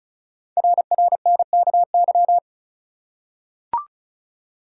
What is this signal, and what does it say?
Morse code, heard as a keyed tone, about 36 wpm, 700 Hz. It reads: RRNKY
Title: Code Groups - 5 Letters - 35wpm